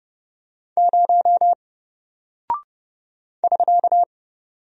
Morse code, heard as a keyed tone, about 31 wpm, 700 Hz. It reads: 0 <SK>
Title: Single Character (Letter:Number:Punctuation:Prosign) Alt 30wpm